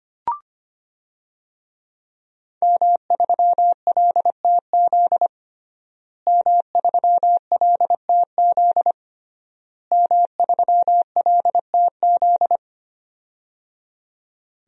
Morse code, heard as a keyed tone, about 25 wpm, 700 Hz. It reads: M3LTZ M3LTZ M3LTZ
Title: DXCC-100 Call Signs - 5 Characters - Repeated 3x - 25wpm